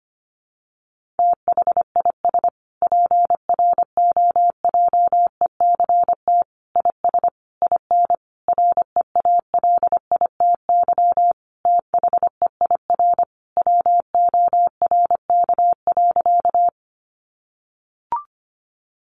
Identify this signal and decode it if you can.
Morse code, heard as a keyed tone, about 25 wpm, 700 Hz. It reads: THIS PROJECT IS IN REALITY THEIR WORK.
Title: Sentences from Top 1400 Words 25wpm